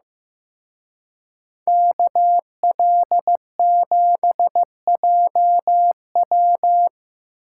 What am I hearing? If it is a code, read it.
Morse code, heard as a keyed tone, about 18 wpm, 700 Hz. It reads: KL7JW
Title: Speed Racing - USA 2x2 Call Signs 15wpm